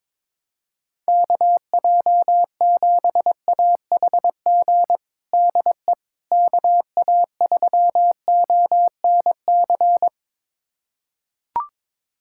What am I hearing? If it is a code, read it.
Morse code, heard as a keyed tone, about 21 wpm, 700 Hz. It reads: KJ7AHG DE KA3ONC
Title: USA Callsign DE Callsign 22wpm